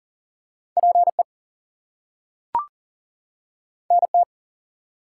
Morse code, heard as a keyed tone, about 40 wpm, 700 Hz. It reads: PE NT
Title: US States and Canadian Provinces 40wpm